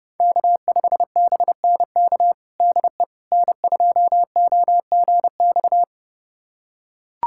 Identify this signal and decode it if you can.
Morse code, heard as a keyed tone, about 30 wpm, 700 Hz. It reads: K5BNK DE N2OGX
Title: USA Callsign DE Callsign - 5 Characters - 30wpm